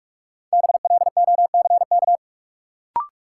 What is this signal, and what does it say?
Morse code, heard as a keyed tone, about 45 wpm, 700 Hz. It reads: BLOCK
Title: Sentences from Top 1600 Words - New Word Drill - 45wpm